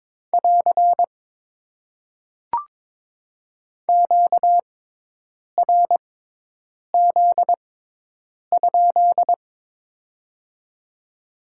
Morse code, heard as a keyed tone, about 22 wpm, spaced as for 8 wpm, 700 Hz. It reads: <AR> QRZ?
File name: Top 90 QSO Elements - 6x Character Spacing - 22wpm